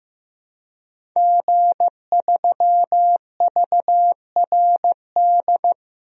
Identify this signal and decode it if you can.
Morse code, heard as a keyed tone, about 15 wpm, 700 Hz. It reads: G3VRD
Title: DXCC-100 Call Signs - 5 Characters - Repeated 2x - 15wpm